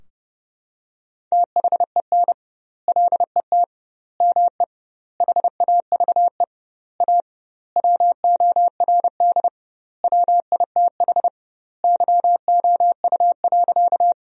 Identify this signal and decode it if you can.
Morse code, heard as a keyed tone, about 30 wpm, 700 Hz. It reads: THEN LET ME HAVE A WORD WITH YOU.
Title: 2x Word-Spacing - Sentences from Top 300 Words - 30wpm